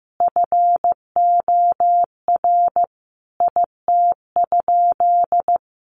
Morse code, heard as a keyed tone, about 15 wpm, 700 Hz. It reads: FOR IT?
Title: Sentences from Top 900 Words 15wpm